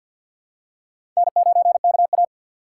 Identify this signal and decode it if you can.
Morse code, heard as a keyed tone, about 48 wpm, 700 Hz. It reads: N9KA
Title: USA Call Signs - 4 Characters - 3x Character Spacing - 50wpm